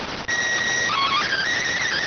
SSTV (slow-scan television), PD 290.